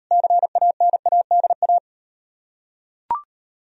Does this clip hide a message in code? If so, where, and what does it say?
Yes: Morse code from 0.1 to 1.8 s, heard as a keyed tone, about 38 wpm, 700 Hz. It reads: CANADA